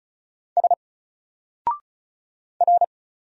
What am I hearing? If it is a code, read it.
Morse code, heard as a keyed tone, about 35 wpm, 700 Hz. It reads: S R
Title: Single Letter-Number 35wpm